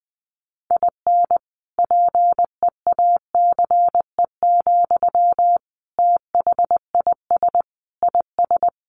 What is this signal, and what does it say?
Morse code, heard as a keyed tone, about 20 wpm, 700 Hz. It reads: IN PEACE, THIS IS
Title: Sentences from Top 1500 Words 20wpm